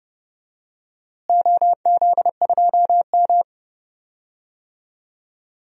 Morse code, heard as a keyed tone, about 30 wpm, 700 Hz. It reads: OZ2M
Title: DXCC-100 Call Signs - 4 Characters - Repeated 2x - 30wpm